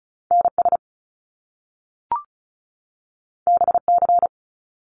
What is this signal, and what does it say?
Morse code, heard as a keyed tone, about 36 wpm, 700 Hz. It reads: NS BC
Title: Canadian Provinces and Territories 35wpm